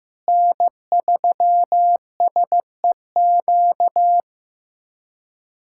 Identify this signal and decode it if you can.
Morse code, heard as a keyed tone, about 15 wpm, 700 Hz. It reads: N3SEQ
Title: USA Callsign DE Callsign 15wpm